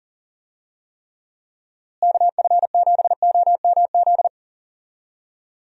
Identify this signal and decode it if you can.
Morse code, heard as a keyed tone, about 40 wpm, 700 Hz. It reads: KF7OMZ
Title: USA Call Signs - 6 Characters - Repeated 2x - 40wpm